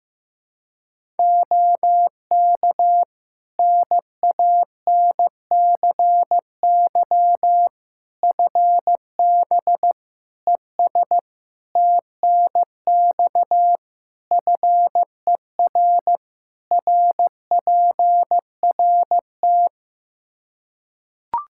Morse code, heard as a keyed tone, about 15 wpm, 700 Hz. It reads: OK NANCY FB ES TNX FER RPRT
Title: Practice QSO - Exchange 2 - Info - 15wpm